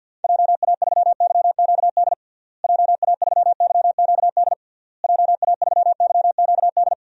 Morse code, heard as a keyed tone, about 48 wpm, 700 Hz. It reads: JA3YQD JA3YQD JA3YQD
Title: DXCC-100 Call Signs - 6 Characters - Repeated 3x - 50wpm